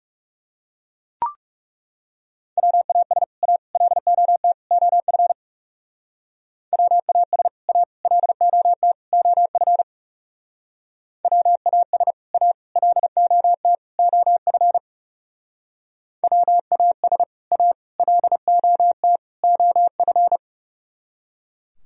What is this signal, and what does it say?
Morse code, heard as a keyed tone, about 38 wpm, 700 Hz. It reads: WAS A LOT OF WAS A LOT OF WAS A LOT OF WAS A LOT OF